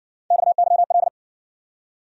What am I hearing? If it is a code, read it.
Morse code, heard as a keyed tone, about 53 wpm, 700 Hz. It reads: XYL